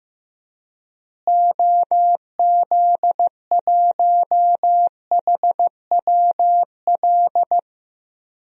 Morse code, heard as a keyed tone, about 15 wpm, 700 Hz. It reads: OZ1HWL
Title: DXCC-100 Call Signs - 6 Characters - Repeated 3x - 15wpm